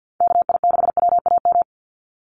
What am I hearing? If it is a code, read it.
Morse code, heard as a keyed tone, about 50 wpm, 700 Hz. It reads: KI6WAM
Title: USA Call Signs 50wpm